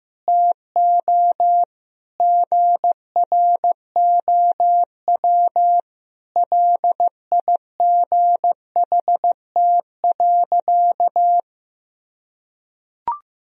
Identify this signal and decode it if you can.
Morse code, heard as a keyed tone, about 15 wpm, 700 Hz. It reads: TO GROW LIGHT.
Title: Sentences from Top 1000 Words - Review Part 1 - 15wpm